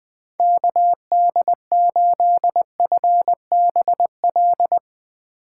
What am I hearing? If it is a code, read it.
Morse code, heard as a keyed tone, about 20 wpm, 700 Hz. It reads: KD8FBL
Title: USA Call Signs - 3x Character Spacing - 20wpm